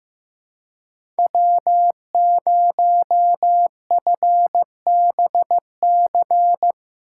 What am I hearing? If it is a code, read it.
Morse code, heard as a keyed tone, about 15 wpm, 700 Hz. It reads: W0FBC
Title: USA Call Signs - 5 Characters - 2x Character Spacing - 15wpm